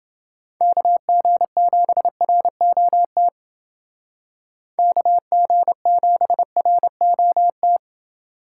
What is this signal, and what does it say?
Morse code, heard as a keyed tone, about 30 wpm, 700 Hz. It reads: KG7ROT KG7ROT